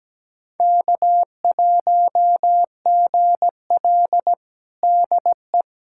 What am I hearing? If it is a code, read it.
Morse code, heard as a keyed tone, about 17 wpm, 700 Hz. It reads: K1GL DE
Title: USA Callsign DE Callsign - 4 Characters - 17wpm